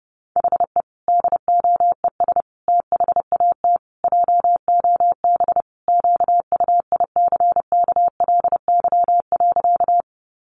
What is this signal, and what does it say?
Morse code, heard as a keyed tone, about 30 wpm, 700 Hz. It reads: HE DOES THAT JOB QUICKLY.
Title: Sentences from Top 700 Words 30wpm